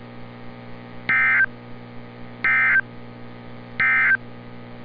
Emergency Alert System (SAME) tones, heard as harsh buzzing bursts. It is heard from 1.0 to 4.2 s.
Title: NOAA Weather Radio - Altoona, PA